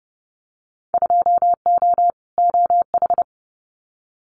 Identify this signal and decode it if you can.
Morse code, heard as a keyed tone, about 30 wpm, 700 Hz. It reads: 2O OH
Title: Winter Field Day Exchanges 30wpm